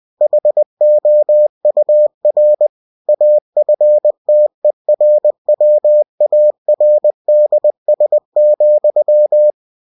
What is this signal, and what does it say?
Morse code, heard as a keyed tone, about 20 wpm, 600 Hz. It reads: HOUR AFTERWARDS,
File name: cw-books / winnie-the-pooh / winnie-20wpm-page-0032